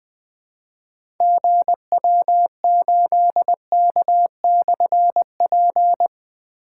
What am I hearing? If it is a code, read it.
Morse code, heard as a keyed tone, about 20 wpm, 700 Hz. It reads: GW8K/P
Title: Difficult Call Signs 20wpm